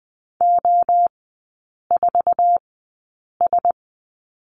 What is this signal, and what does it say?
Morse code, heard as a keyed tone, about 20 wpm, spaced as for 8 wpm, 700 Hz. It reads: O4S
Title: 047 Lesson 9e - Number 4 - Three Characters 20wpm